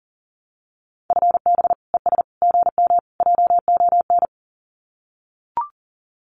Morse code, heard as a keyed tone, about 40 wpm, 700 Hz. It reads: FB ES GM JON
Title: Common QSO Phrases Encore 40wpm